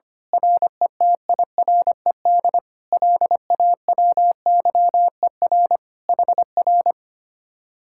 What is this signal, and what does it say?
Morse code, heard as a keyed tone, about 25 wpm, 700 Hz. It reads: RETIRED LAWYER HR
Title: Speed-Racing Common QSO Phrases Encore - 25wpm